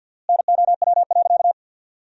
Morse code, heard as a keyed tone, about 49 wpm, 700 Hz. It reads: NOW.
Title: 4x Word-Spacing - Sentences from Top 300 Words - 50wpm